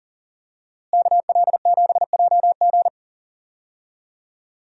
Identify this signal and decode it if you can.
Morse code, heard as a keyed tone, about 40 wpm, 700 Hz. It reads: KL7JG
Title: DXCC-100 Call Signs - 5 Characters - Repeated 2x - 40wpm